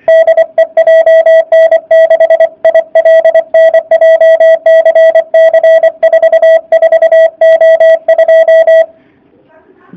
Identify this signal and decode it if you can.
Morse code, heard as a keyed tone, about 24 wpm, 660 Hz. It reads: DE JN6ILN JCC44O2
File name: jh8cbh / 2019cw / sp29- w